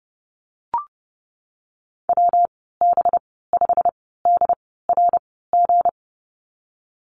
Morse code, heard as a keyed tone, about 30 wpm, spaced as for 17 wpm, 700 Hz. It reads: WB5DRG
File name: Lesson 37g - Letter x - Call Signs 30wpm